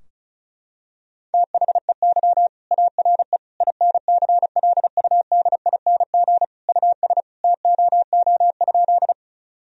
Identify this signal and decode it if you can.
Morse code, heard as a keyed tone, about 36 wpm, 700 Hz. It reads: THEY ARE INCLUDING US TOO?
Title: Sentences from Top 1000 Words - Review Part 4 - 35wpm